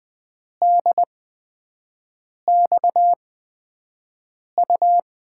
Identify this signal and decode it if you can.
Morse code, heard as a keyed tone, about 20 wpm, spaced as for 5 wpm, 700 Hz. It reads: DXU
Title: USA Call Signs - 6 Characters - 8x Character Spacing - 20wpm